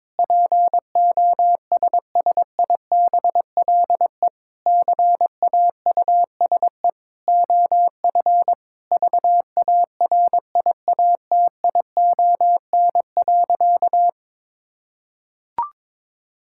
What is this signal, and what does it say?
Morse code, heard as a keyed tone, about 22 wpm, 700 Hz. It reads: POSSIBLE CAUSE OF VARIATION.
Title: Sentences from Top 1600 Words 22wpm